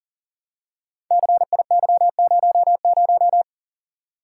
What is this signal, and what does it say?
Morse code, heard as a keyed tone, about 38 wpm, 690 Hz. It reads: CIY00
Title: Code Groups - 5 Letter:Numbers - 40wpm